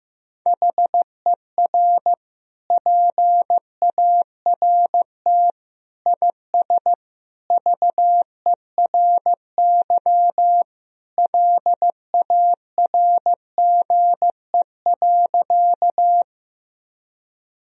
Morse code, heard as a keyed tone, about 15 wpm, 700 Hz. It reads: HER PART IS VERY LARGE.
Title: Sentences from Top 1000 Words - Review Part 1 - 15wpm